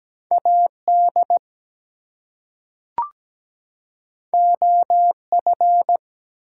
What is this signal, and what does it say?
Morse code, heard as a keyed tone, about 17 wpm, 700 Hz. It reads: AD OF
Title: Two Letter Words - 17wpm